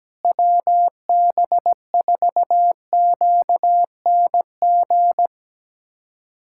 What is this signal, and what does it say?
Morse code, heard as a keyed tone, about 17 wpm, 700 Hz. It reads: WB4QNG
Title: USA Callsign DE Callsign - 6 Characters - 17wpm